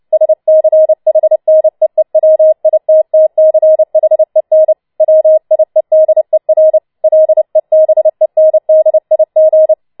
Morse code, heard as a keyed tone, about 29 wpm, 630 Hz. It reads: SCHNEEWITTCHEN WIEDER LEBENDIG